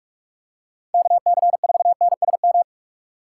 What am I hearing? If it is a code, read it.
Morse code, heard as a keyed tone, about 46 wpm, 700 Hz. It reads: KC4NSM
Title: USA Call Signs - 6 Characters - 3x Character Spacing - 45wpm